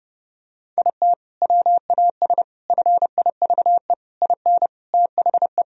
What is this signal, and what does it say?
Morse code, heard as a keyed tone, about 32 wpm, 700 Hz. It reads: IT WAS FIVE IN THE
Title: Sentences from Top 1000 Words - Review Part 2 - 30wpm